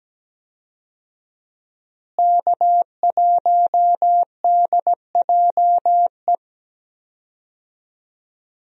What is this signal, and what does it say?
Morse code, heard as a keyed tone, about 17 wpm, 700 Hz. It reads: K1DJE
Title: USA Call Signs - 5 Characters - Repeated 3x - 17wpm